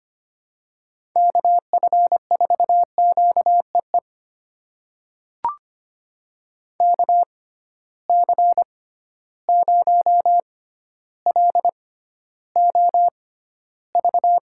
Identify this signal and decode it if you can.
Morse code, heard as a keyed tone, about 25 wpm, spaced as for 13 wpm, 700 Hz. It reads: KF4QEE KC0LOV
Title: USA Call Signs - 6 Characters - 6x Character Spacing - 25wpm